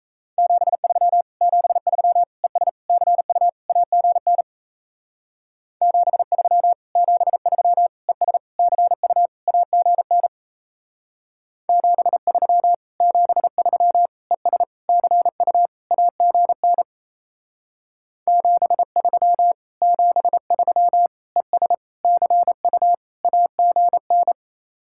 Morse code, heard as a keyed tone, about 37 wpm, 700 Hz. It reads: 73 73 ES CU AGN 73 73 ES CU AGN 73 73 ES CU AGN 73 73 ES CU AGN